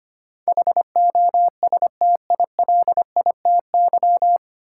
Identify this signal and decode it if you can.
Morse code, heard as a keyed tone, about 25 wpm, 700 Hz. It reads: HOSTILITY
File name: Nine Letter Words - 25wpm